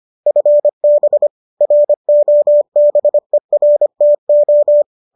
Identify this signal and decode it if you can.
Morse code, heard as a keyed tone, about 25 wpm, 570 Hz. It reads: FB ROBERTO